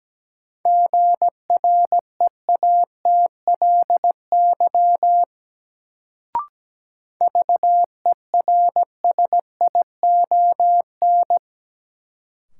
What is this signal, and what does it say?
Morse code, heard as a keyed tone, about 17 wpm, 700 Hz. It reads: GREATLY VERSION
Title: Sentences from Top 1400 Words - New Word Drill - 17wpm